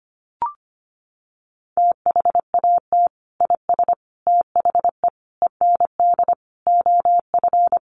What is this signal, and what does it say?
Morse code, heard as a keyed tone, about 25 wpm, 700 Hz. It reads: THAT IS THE END OF
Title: Sentences from Top 200 Words 25wpm